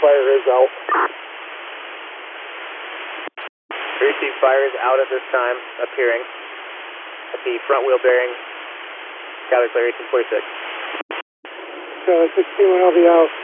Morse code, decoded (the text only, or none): none